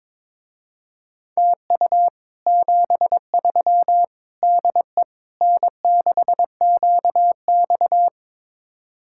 Morse code, heard as a keyed tone, about 22 wpm, 700 Hz. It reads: TU 73 DE N6QX
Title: Speed-Racing Common QSO Phrases Encore - 22wpm